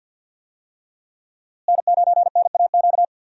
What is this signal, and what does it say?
Morse code, heard as a keyed tone, about 50 wpm, 700 Hz. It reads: N9NAQ
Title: USA Call Signs - 5 Characters - Repeated 3x - 50wpm